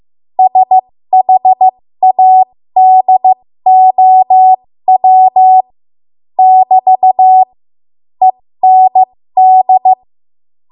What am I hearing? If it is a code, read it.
Morse code, heard as a keyed tone, about 15 wpm, spaced as for 13 wpm, 750 Hz. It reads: SHADOW <BT> END
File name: Morse / 13WPM